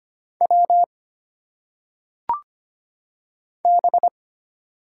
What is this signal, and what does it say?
Morse code, heard as a keyed tone, about 25 wpm, 700 Hz. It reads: W B
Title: Single Letters - Phonetic Pronunciation 25wpm